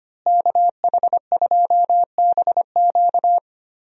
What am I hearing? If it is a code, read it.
Morse code, heard as a keyed tone, about 25 wpm, 700 Hz. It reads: KH2BQ
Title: USA Call Signs - 5 Characters - Repeated 2x - 25wpm